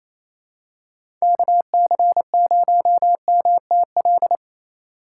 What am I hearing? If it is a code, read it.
Morse code, heard as a keyed tone, about 27 wpm, 700 Hz. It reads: KC0MTL
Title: USA Call Signs - 6 Characters - 3x Character Spacing - 28wpm